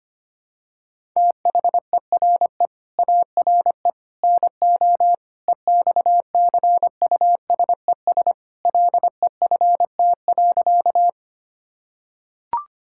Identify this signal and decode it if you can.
Morse code, heard as a keyed tone, about 25 wpm, 700 Hz. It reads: THERE ARE NO EXCUSES LEFT.
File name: practice / 25wpm / Sentences with 3-Word Phrases - 25wpm